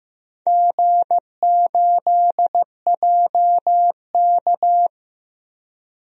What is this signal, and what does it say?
Morse code, heard as a keyed tone, about 15 wpm, 700 Hz. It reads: G8JK
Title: DXCC-100 Call Signs - 4 Characters - 15wpm